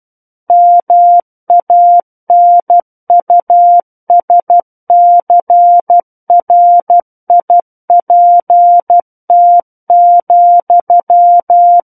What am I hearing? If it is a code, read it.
Morse code, heard as a keyed tone, about 12 wpm, 700 Hz. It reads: MANUSCRIPT,
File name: Book One / 03-Chapter 3 - 12 WPM